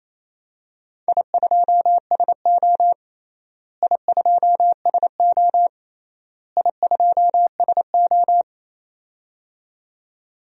Morse code, heard as a keyed tone, about 28 wpm, 700 Hz. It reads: I2SO I2SO I2SO
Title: DXCC-100 Call Signs - 4 Characters - Repeated 3x - 28wpm